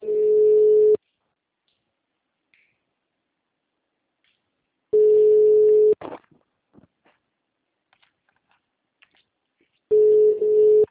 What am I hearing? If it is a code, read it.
Ringback tone (Europe, 425 Hz).